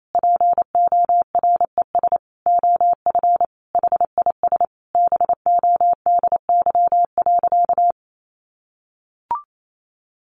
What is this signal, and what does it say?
Morse code, heard as a keyed tone, about 28 wpm, 700 Hz. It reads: PORES OF HIS BODY.
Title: A Sound Of Thunder 28wpm